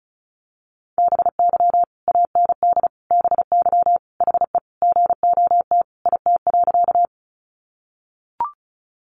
Morse code, heard as a keyed tone, about 34 wpm, 700 Hz. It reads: BY AND BY HE GOT IT.
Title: Sentences from Top 300 Words 35wpm